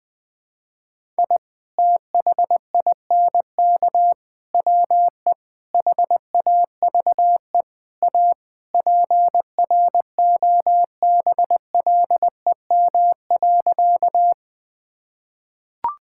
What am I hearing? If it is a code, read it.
Morse code, heard as a keyed tone, about 21 wpm, 700 Hz. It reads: I THINK WE HAVE A PROBLEM.
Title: Sentences with 4-Word Phrases - Spoken Beginning - 20wpm